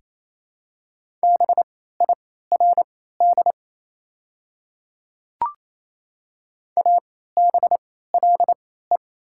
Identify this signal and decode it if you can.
Morse code, heard as a keyed tone, about 28 wpm, spaced as for 16 wpm, 690 Hz. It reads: BIRD ABLE
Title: 3x Character Spacing - Top 500 Words - 28wpm